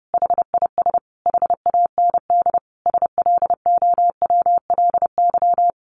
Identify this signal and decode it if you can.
Morse code, heard as a keyed tone, about 30 wpm, 700 Hz. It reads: HIS HAND SLOWLY
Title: A Sound Of Thunder 30wpm